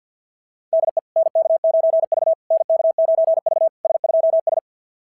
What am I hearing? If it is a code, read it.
Morse code, heard as a keyed tone, about 53 wpm, 690 Hz. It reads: DE NK9V NK9V S2S
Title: SOTA QSO Practice 50wpm